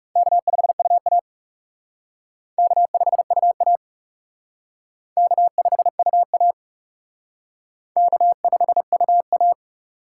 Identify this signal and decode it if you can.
Morse code, heard as a keyed tone, about 36 wpm, 700 Hz. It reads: K5UA K5UA K5UA K5UA
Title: Speed Racing - USA 1x2 Call Signs 30wpm